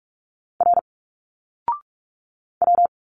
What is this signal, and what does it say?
Morse code, heard as a keyed tone, about 46 wpm, 700 Hz. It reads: R W
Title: Single Letters 45wpm